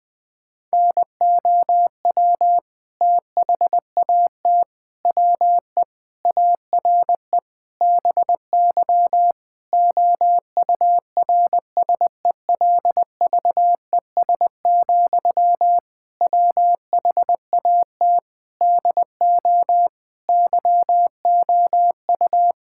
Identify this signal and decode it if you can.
Morse code, heard as a keyed tone, about 21 wpm, 700 Hz. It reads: NOW THAT WE ARE BY OURSELVES, WHAT DO YOU
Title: Sentences from Top 1600 Words 20wpm